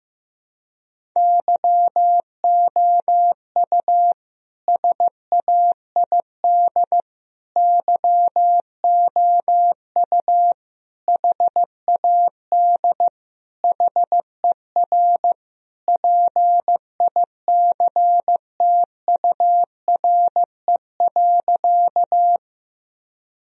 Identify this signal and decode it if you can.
Morse code, heard as a keyed tone, about 14 wpm, 700 Hz. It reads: YOU SAID YOU HAD HER PICTURE.
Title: Sentences from Top 1000 Words - Review Part 1 - 15wpm